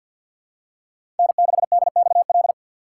Morse code, heard as a keyed tone, about 51 wpm, 700 Hz. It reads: N6DXL
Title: USA Call Signs - 5 Characters - 2x Character Spacing - 50wpm